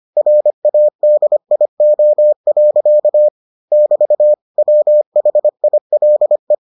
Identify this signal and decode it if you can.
Morse code, heard as a keyed tone, about 25 wpm, 600 Hz. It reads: RADIO. <BT> WHILE